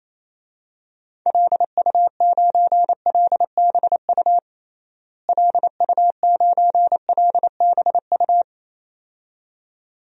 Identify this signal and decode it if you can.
Morse code, heard as a keyed tone, about 28 wpm, 700 Hz. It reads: LU9LBU LU9LBU
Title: DXCC-100 Call Signs - 6 Characters - Repeated 2x - 28wpm